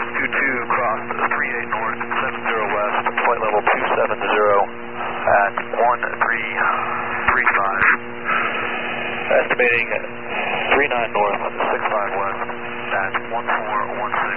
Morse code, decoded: none

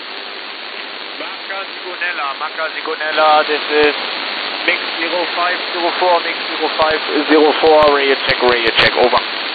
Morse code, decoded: none

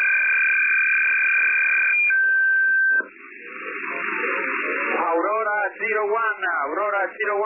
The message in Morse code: none

